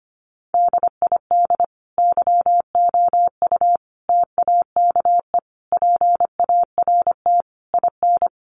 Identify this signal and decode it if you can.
Morse code, heard as a keyed tone, about 25 wpm, 700 Hz. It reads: DID YOU TAKE PART IN
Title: Sentences from Top 1200 Words 25wpm